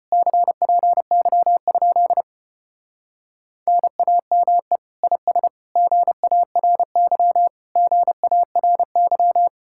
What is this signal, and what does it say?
Morse code, heard as a keyed tone, about 33 wpm, 700 Hz. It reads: CPY? NAME IS GARY GARY